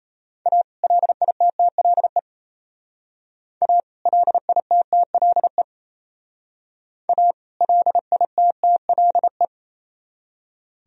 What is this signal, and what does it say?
Morse code, heard as a keyed tone, about 34 wpm, 700 Hz. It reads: A LITTLE A LITTLE A LITTLE
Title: Speed-Racing - Sets of 2 Words 28wpm